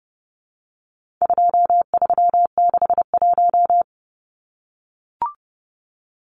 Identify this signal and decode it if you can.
Morse code, heard as a keyed tone, about 30 wpm, 700 Hz. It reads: 2361
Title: Numbers - 4-Digit 30wpm